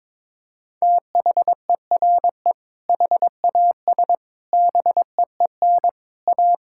Morse code, heard as a keyed tone, about 22 wpm, 700 Hz. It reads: THERE HAS BEEN A
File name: practice / 22wpm / Sets of 8 Words - 22wpm